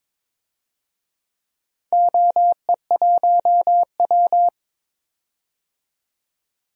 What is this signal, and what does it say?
Morse code, heard as a keyed tone, about 22 wpm, 700 Hz. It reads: OE1W
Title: DXCC-100 Call Signs - 4 Characters - 22wpm